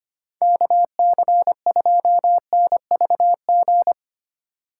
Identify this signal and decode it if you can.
Morse code, heard as a keyed tone, about 25 wpm, 700 Hz. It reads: KC2NVG
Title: USA Call Signs - 6 Characters - 3x Character Spacing - 25wpm